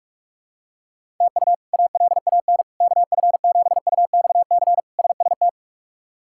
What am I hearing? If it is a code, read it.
Morse code, heard as a keyed tone, about 48 wpm, 700 Hz. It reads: TU ALAN KF7UXC SST
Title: K1USN SST QSO Practice - 45wpm